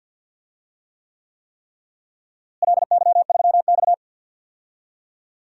Morse code, heard as a keyed tone, about 51 wpm, 700 Hz. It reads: LY3X
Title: DXCC-100 Call Signs - 4 Characters - Repeated 3x - 50wpm